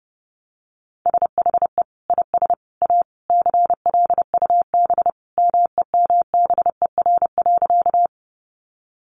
Morse code, heard as a keyed tone, about 30 wpm, 700 Hz. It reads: SHE IS A CLUB MEMBER.
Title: Sentences from Top 1200 Words 30wpm